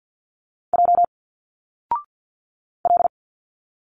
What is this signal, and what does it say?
Morse code, heard as a keyed tone, about 50 wpm, 700 Hz. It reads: J L
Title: Single Letters 50wpm